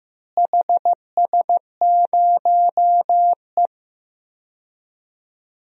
Morse code, heard as a keyed tone, about 15 wpm, 700 Hz. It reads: HS0E
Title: DXCC-100 Call Signs - 4 Characters - Repeated 3x - 15wpm